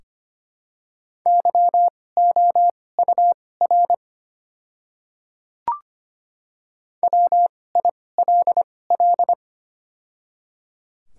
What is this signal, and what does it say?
Morse code, heard as a keyed tone, about 25 wpm, spaced as for 18 wpm, 700 Hz. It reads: YOUR WILL
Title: 2x Character Spacing - Top 200 Words - 25wpm